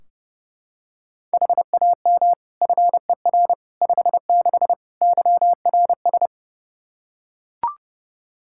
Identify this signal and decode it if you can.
Morse code, heard as a keyed tone, about 32 wpm, 700 Hz. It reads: HAM FER 56 YRS
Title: Practice QSO - Exchange 3 - More Info - 30wpm